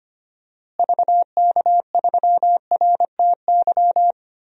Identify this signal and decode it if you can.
Morse code, heard as a keyed tone, about 25 wpm, 700 Hz. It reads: VK3RTY
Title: DXCC-100 Call Signs - 6 Characters - Repeated 3x - 25wpm